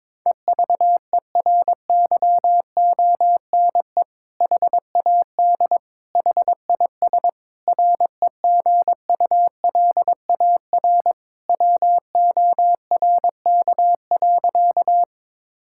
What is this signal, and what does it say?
Morse code, heard as a keyed tone, about 22 wpm, 700 Hz. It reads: EVERYONE HAD HIS REGULAR WORK.
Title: Sentences from Top 1000 Words - Review Part 6 - 22wpm